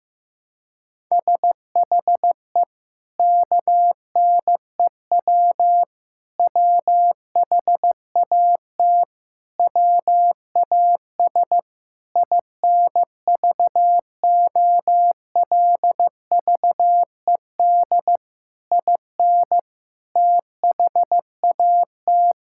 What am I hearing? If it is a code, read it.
Morse code, heard as a keyed tone, about 15 wpm, 700 Hz. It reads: SHE KNEW WHAT WAS INVOLVED IN THAT
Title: Sentences from Top 1000 Words - Review Part 5 - 15wpm